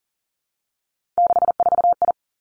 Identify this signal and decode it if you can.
Morse code, heard as a keyed tone, about 40 wpm, 700 Hz. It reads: 64I
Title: Three Challenging Letter-Number Combinations 40wpm